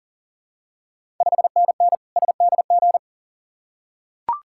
Morse code, heard as a keyed tone, about 40 wpm, 700 Hz. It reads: 5NN SDG
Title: Ontario QSO Party Exchange 40wpm